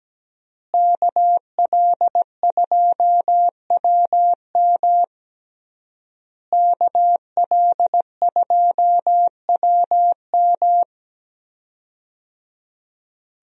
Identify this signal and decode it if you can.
Morse code, heard as a keyed tone, about 17 wpm, 700 Hz. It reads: KL2WM KL2WM
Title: USA Call Signs - 5 Characters - Repeated 3x - 17wpm